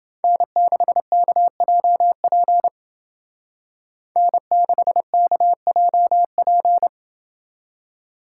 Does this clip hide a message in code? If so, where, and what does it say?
Yes: Morse code from 0.2 to 6.9 s, heard as a keyed tone, about 29 wpm, 700 Hz. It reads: N6KJP N6KJP